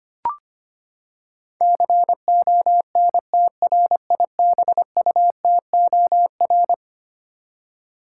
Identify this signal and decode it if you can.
Morse code, heard as a keyed tone, about 25 wpm, 700 Hz. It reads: CONTRIBUTOR
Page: morse-code-ninja-practice-25wpm